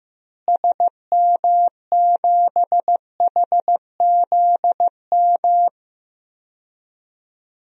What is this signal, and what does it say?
Morse code, heard as a keyed tone, about 15 wpm, 700 Hz. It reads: SM7HZM
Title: DXCC-100 Call Signs - 6 Characters - Repeated 3x - 15wpm